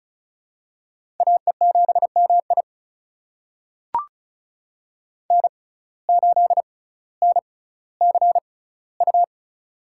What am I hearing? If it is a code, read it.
Morse code, heard as a keyed tone, about 34 wpm, spaced as for 18 wpm, 700 Hz. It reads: AE7MI N8NCU
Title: USA Call Signs - 5 Characters - 6x Character Spacing - 35wpm